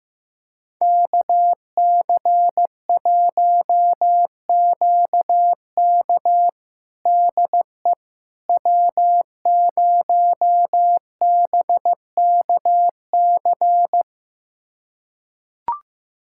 Morse code, heard as a keyed tone, about 15 wpm, 700 Hz. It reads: KC1QK DE W0BKC
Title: USA Callsign DE Callsign - 5 Characters - 15wpm